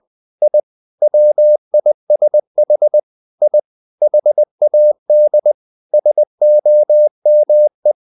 Morse code, heard as a keyed tone, about 20 wpm, 600 Hz. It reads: I WISH I HAD SOME
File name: the old man and the sea / the old man and the sea part 1 20wpm